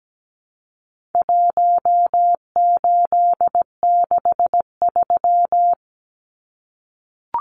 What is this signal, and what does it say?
Morse code, heard as a keyed tone, about 17 wpm, 700 Hz. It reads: 1863
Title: Numbers - Contest Style - 4-Digit 17wpm